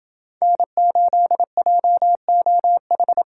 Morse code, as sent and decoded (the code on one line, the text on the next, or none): -. ---.. .--- --- ....
N8JOH